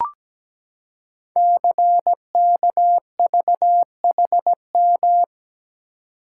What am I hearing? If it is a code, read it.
Morse code, heard as a keyed tone, about 17 wpm, 700 Hz. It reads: CKVHM
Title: Code Groups - 5 Letters - 17wpm